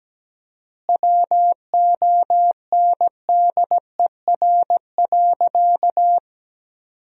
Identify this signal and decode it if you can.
Morse code, heard as a keyed tone, about 17 wpm, 700 Hz. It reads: WONDER.
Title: 4x Word-Spacing - Sentences from Top 500 Words - 17wpm